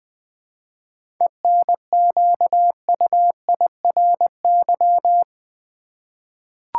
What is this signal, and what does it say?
Morse code, heard as a keyed tone, about 20 wpm, 690 Hz. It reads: ENQUIRY
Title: 7 Letter Words - First 3 Letters Given - 20wpm